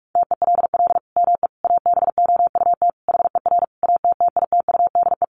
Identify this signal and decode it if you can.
Morse code, heard as a keyed tone, about 44 wpm, 700 Hz. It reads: TELL ME ABOUT HER ATTITUDE
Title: Sentences from Top 1300 Words 45wpm